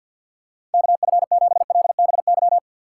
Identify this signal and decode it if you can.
Morse code, heard as a keyed tone, about 52 wpm, 690 Hz. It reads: KF7LBY
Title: USA Call Signs - 4x Character Spacing - 50wpm